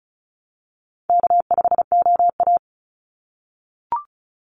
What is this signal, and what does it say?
Morse code, heard as a keyed tone, about 34 wpm, 700 Hz. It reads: K5OA
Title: Call Signs - 35wpm